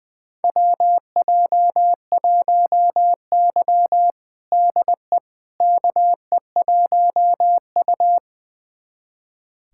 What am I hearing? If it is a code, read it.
Morse code, heard as a keyed tone, about 20 wpm, 700 Hz. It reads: WJ1Y DE KE1U
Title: USA Callsign DE Callsign - 4 Characters - 20wpm